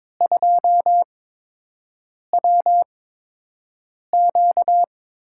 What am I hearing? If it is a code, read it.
Morse code, heard as a keyed tone, about 22 wpm, spaced as for 6 wpm, 700 Hz. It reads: 2WQ
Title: USA Call Signs - 8x Character Spacing - 22wpm